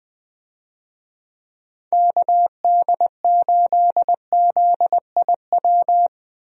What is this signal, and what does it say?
Morse code, heard as a keyed tone, about 20 wpm, 700 Hz. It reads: KD8ZIW
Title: USA Call Signs - 6 Characters - Repeated 2x - 20wpm